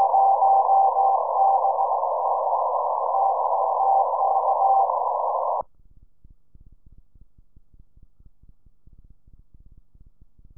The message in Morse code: none